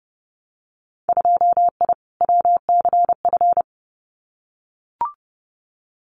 Morse code, heard as a keyed tone, about 32 wpm, 700 Hz. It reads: 2I WCF
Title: Winter Field Day Exchanges 30wpm